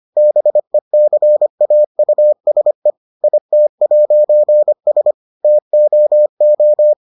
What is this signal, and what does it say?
Morse code, heard as a keyed tone, about 25 wpm, 600 Hz. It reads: BECAUSE IT'S TOO